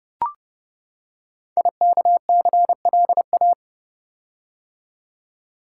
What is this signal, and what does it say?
Morse code, heard as a keyed tone, about 30 wpm, 700 Hz. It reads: IKCLA
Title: Code Groups - 5 Letter:Numbers - 30wpm